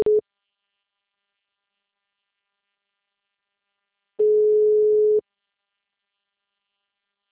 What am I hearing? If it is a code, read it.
Ringback tone (Europe, 425 Hz).